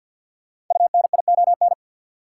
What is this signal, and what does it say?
Morse code, heard as a keyed tone, about 52 wpm, 700 Hz. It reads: UNION
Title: Sentences from Top 1400 Words - New Word Drill - 50wpm